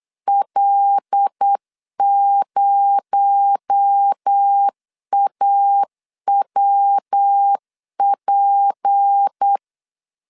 Morse code, heard as a keyed tone, about 8 wpm, 800 Hz. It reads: L0AWP
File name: 8wpm